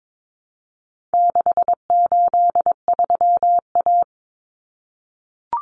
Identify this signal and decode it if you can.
Morse code, heard as a keyed tone, about 22 wpm, 700 Hz. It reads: 683A
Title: Common Cut Numbers - 4-Digit 22wpm